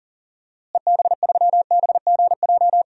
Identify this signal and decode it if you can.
Morse code, heard as a keyed tone, about 40 wpm, 700 Hz. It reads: EB3BGJ
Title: DXCC-100 Call Signs - 6 Characters - Repeated 3x - 40wpm